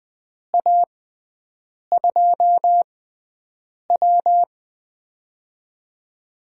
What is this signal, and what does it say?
Morse code, heard as a keyed tone, about 19 wpm, spaced as for 7 wpm, 700 Hz. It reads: A2W
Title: USA Call Signs - 4 Characters - 6x Character Spacing - 20wpm